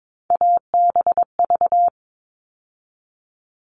Morse code, heard as a keyed tone, about 22 wpm, 700 Hz. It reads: ABV
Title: All Cut Numbers - 3-Digit 22wpm